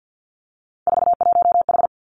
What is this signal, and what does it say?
Morse code, heard as a keyed tone, about 50 wpm, 700 Hz. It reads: 41H
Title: Three Challenging Letter-Number Combinations 50wpm